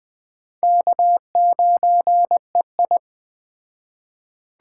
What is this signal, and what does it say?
Morse code, heard as a keyed tone, about 21 wpm, 700 Hz. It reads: K9EI
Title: Speed Racing - USA 1x2 Call Signs 17wpm